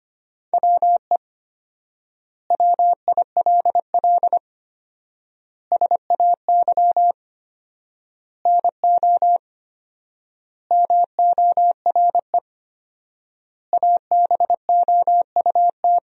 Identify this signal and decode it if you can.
Morse code, heard as a keyed tone, about 25 wpm, 700 Hz. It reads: WE WILL SAY NO MORE ABOUT
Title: 4x Word-Spacing - Sentences from Top 200 Words - 25wpm